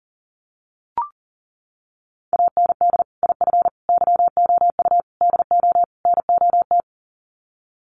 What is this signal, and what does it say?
Morse code, heard as a keyed tone, about 40 wpm, 700 Hz. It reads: AND IF YOU DO NOT
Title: Sets of 5 Words 40wpm